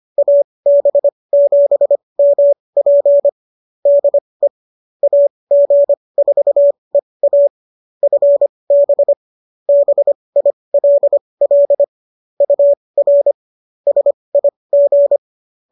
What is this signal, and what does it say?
Morse code, heard as a keyed tone, about 25 wpm, spaced as for 20 wpm, 570 Hz. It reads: AB7MP DE AG4EA FB BILL UR SIG